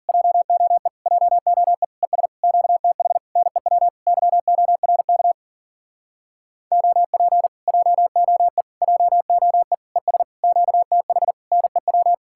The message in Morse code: JOE JOE ES QTH NEW YORK OP JOE JOE ES QTH NEW

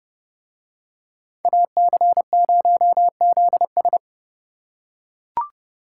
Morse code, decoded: AC0ZS

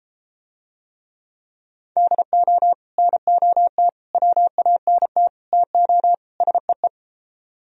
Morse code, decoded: DO NOT WANT TO SEE

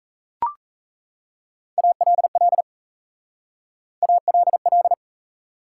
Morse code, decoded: ALL ALL

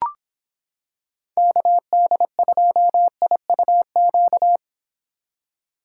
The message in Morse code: KD2IUQ

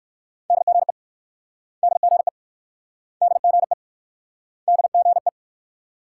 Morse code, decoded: DGE DGE DGE DGE

